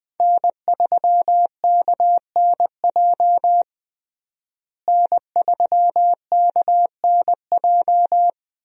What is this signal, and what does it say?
Morse code, heard as a keyed tone, about 21 wpm, 700 Hz. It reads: N3KNJ N3KNJ
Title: USA Call Signs - 5 Characters - Repeated 2x - 20wpm